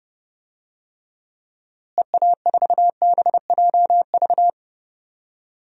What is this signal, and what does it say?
Morse code, heard as a keyed tone, about 32 wpm, 700 Hz. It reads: EA4BJV
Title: DXCC-100 Call Signs - 6 Characters - Repeated 3x - 30wpm